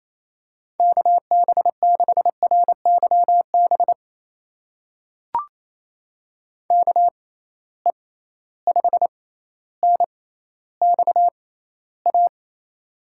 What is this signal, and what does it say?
Morse code, heard as a keyed tone, about 28 wpm, spaced as for 14 wpm, 700 Hz. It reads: KB6RYB KE5NXA
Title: USA Call Signs - 6 Characters - 6x Character Spacing - 28wpm